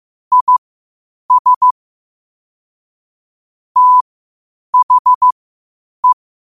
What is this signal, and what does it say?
Morse code, heard as a keyed tone, about 15 wpm, spaced as for 8 wpm, 1000 Hz. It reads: IS THE
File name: qotdfiles / 8wpm